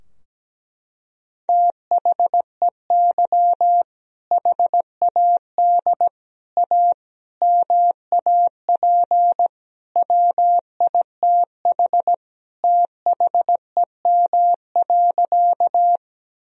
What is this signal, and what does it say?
Morse code, heard as a keyed tone, about 17 wpm, 700 Hz. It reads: THEY HAD A MAP WITH THEM.
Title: Sentences from Top 1000 Words - Review Part 2 - 17wpm